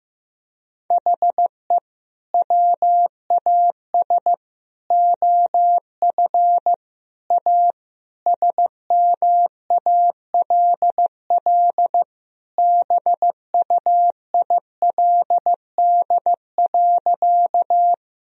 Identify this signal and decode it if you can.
Morse code, heard as a keyed tone, about 15 wpm, 700 Hz. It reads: HE WAS OF A SMALL BUILD.
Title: Sentences from Top 1000 Words - Review Part 1 - 15wpm